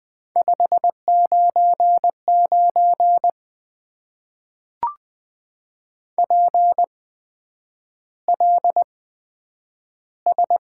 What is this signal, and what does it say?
Morse code, heard as a keyed tone, about 20 wpm, spaced as for 9 wpm, 700 Hz. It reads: 599 PLS
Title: Top 90 QSO Elements - 8x Character Spacing - 20wpm